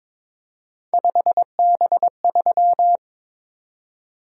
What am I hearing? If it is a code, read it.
Morse code, heard as a keyed tone, about 26 wpm, 700 Hz. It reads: 5B3
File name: Speed-Racing - Three Letter-Number Combinations - 22wpm